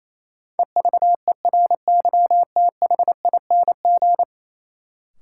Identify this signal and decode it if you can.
Morse code, heard as a keyed tone, about 28 wpm, 700 Hz. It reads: EVERYTHING
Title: Sentences from Top 700 Words - New Word Drill - 28wpm